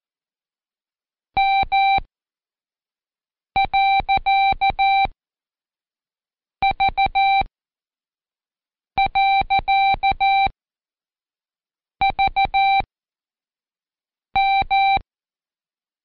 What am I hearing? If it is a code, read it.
Morse code, heard as a keyed tone, about 14 wpm, spaced as for 5 wpm, 780 Hz. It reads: M.V.VM
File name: Code Quick / AudioDownld1 / Seg63-Day7-Pract PeriodMV